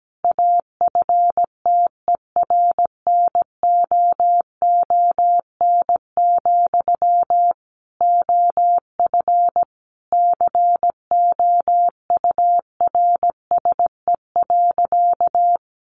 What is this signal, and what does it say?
Morse code, heard as a keyed tone, about 17 wpm, 700 Hz. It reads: AFTERNOON, OF COURSE.
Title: Sentences from Top 1500 Words 17wpm